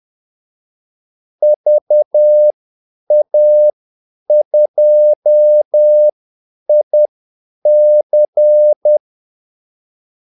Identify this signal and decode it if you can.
Morse code, heard as a keyed tone, about 10 wpm, spaced as for 8 wpm, 600 Hz. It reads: VA2IC
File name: Callsigns-10-8wpm